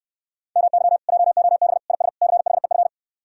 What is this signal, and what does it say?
Morse code, heard as a keyed tone, about 68 wpm, 700 Hz. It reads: MY JOB IS CHEF